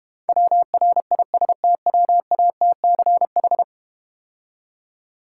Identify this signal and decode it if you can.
Morse code, heard as a keyed tone, about 32 wpm, 700 Hz. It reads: WRISTWATCH